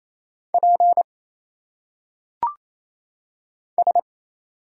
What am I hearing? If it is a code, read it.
Morse code, heard as a keyed tone, about 27 wpm, 700 Hz. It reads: P S
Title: Single Letters - Phonetic Pronunciation 28wpm